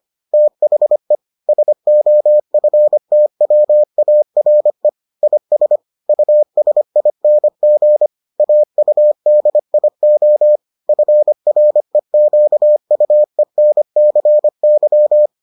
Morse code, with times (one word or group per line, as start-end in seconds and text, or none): 0.3-1.1 THE
1.5-4.9 SOFTWARE
5.2-5.8 IS
6.1-8.1 USING
8.4-10.6 AUDIO
10.9-15.3 FREQUENCY